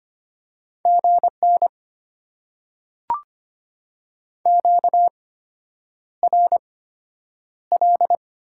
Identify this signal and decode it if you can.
Morse code, heard as a keyed tone, about 25 wpm, spaced as for 9 wpm, 700 Hz. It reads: GN QRL
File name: Top 90 QSO Elements - 8x Character Spacing - 25wpm